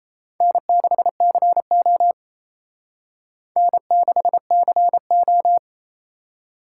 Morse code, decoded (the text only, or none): N6CO N6CO